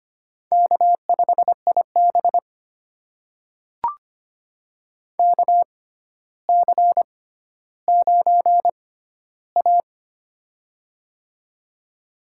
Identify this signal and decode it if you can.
Morse code, heard as a keyed tone, about 25 wpm, spaced as for 13 wpm, 700 Hz. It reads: K5IB KC9A
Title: USA Call Signs - 4 Characters - 6x Character Spacing - 25wpm